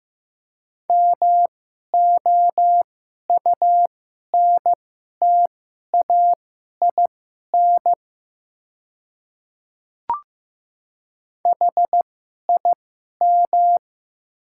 Morse code, heard as a keyed tone, about 15 wpm, spaced as for 11 wpm, 700 Hz. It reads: MOUNTAIN HIM
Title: 2x Character Spacing - Top 300 Words - 15wpm